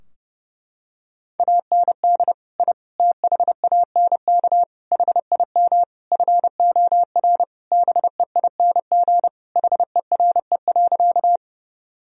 Morse code, heard as a keyed tone, about 30 wpm, 700 Hz. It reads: AND I THANK HIM FOR BEING HERE.
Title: Sentences from Top 1000 Words - Review Part 3 - 30wpm